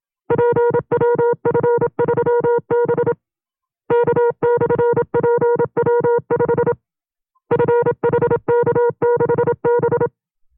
Morse code, heard as a keyed tone, about 27 wpm, 480 Hz. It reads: PWF3B K/PW5 FHK6B
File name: FreeFiles / CW-practice / 25WPM-side1